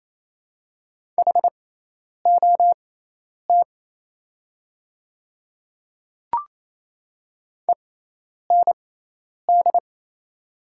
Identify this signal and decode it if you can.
Morse code, heard as a keyed tone, about 27 wpm, spaced as for 10 wpm, 700 Hz. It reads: HOT END
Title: 6x Character Spacing - Top 200 Words - 28wpm